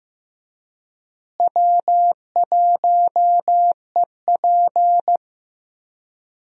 Morse code, heard as a keyed tone, about 16 wpm, 700 Hz. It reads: W1EP
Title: USA Call Signs - 4 Characters - Repeated 3x - 15wpm